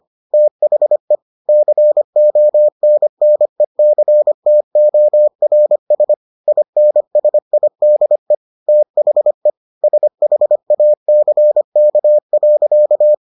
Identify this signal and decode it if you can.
Morse code, heard as a keyed tone, about 25 wpm, 600 Hz. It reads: THE CONNECTORS INSIDE THE SHACK.